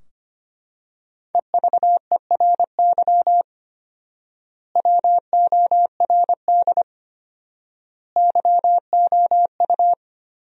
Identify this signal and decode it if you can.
Morse code, heard as a keyed tone, about 25 wpm, 700 Hz. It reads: EVERY WORD YOU
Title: 4x Word-Spacing - Sentences from Top 400 Words - 25wpm